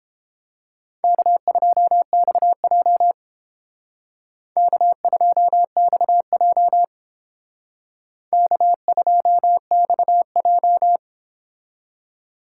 Morse code, heard as a keyed tone, about 28 wpm, 700 Hz. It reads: K2XJ K2XJ K2XJ